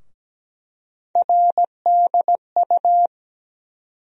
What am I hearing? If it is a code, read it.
Morse code, heard as a keyed tone, about 17 wpm, 700 Hz. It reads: RDU
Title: USA Call Sign Postfixes - 3 Letters 17wpm